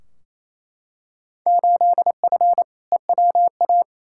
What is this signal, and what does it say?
Morse code, heard as a keyed tone, about 28 wpm, 700 Hz. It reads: 8F EWA
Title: Field Day Exchanges Encore - 28wpm